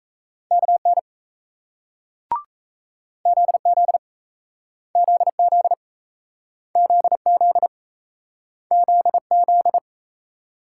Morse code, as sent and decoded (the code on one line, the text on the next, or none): -.- -. / --.. --.. / --.. --.. / --.. --.. / --.. --..
KN ZZ ZZ ZZ ZZ